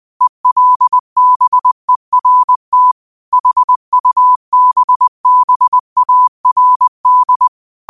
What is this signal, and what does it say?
Morse code, heard as a keyed tone, about 20 wpm, 1000 Hz. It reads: ELBERT HUBBARD
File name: qotdfiles / 20wpm